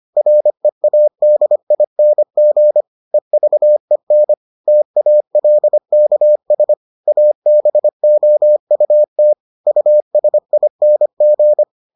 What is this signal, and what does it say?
Morse code, heard as a keyed tone, about 25 wpm, 600 Hz. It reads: READING EVEN TALKS ABOUT USING